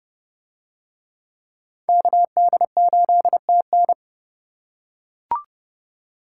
Morse code, heard as a keyed tone, about 27 wpm, 700 Hz. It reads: KD8TN